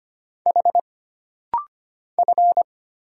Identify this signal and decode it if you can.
Morse code, heard as a keyed tone, about 25 wpm, 700 Hz. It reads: H F
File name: Single Character (Letter:Number:Punctuation:Prosign) - Rapid-Fire - 25wpm